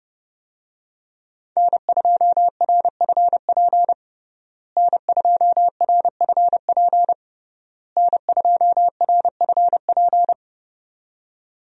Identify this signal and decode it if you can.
Morse code, heard as a keyed tone, about 32 wpm, 700 Hz. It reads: N2RFP N2RFP N2RFP
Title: USA Call Signs - 5 Characters - Repeated 3x - 30wpm